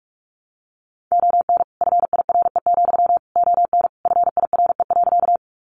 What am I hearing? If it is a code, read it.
Morse code, heard as a keyed tone, about 46 wpm, 700 Hz. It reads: ON FIRE, ON FIRE.
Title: Sentences from Top 400 Words 45wpm